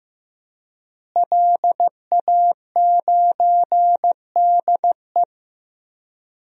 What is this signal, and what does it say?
Morse code, heard as a keyed tone, about 15 wpm, 700 Hz. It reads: LA9DE
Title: DXCC-100 Call Signs - 5 Characters - Repeated 2x - 15wpm